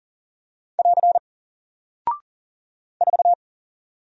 Morse code, heard as a keyed tone, about 39 wpm, 700 Hz. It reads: <AR> 4
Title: Single Character (Letter:Number:Punctuation:Prosign) 40wpm